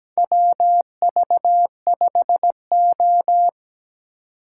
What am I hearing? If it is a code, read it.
Morse code, heard as a keyed tone, about 17 wpm, 700 Hz. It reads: WV5O
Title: USA Call Signs - 4 Characters - 2x Character Spacing - 17wpm